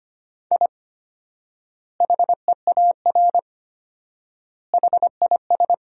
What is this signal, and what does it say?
Morse code, heard as a keyed tone, about 25 wpm, 700 Hz. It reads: I HEAR HIS
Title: 4x Word-Spacing - Sentences from Top 500 Words - 25wpm